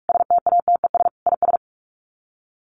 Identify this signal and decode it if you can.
Morse code, heard as a keyed tone, about 44 wpm, 700 Hz. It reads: STATES IS
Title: Sets of 5 Words 45wpm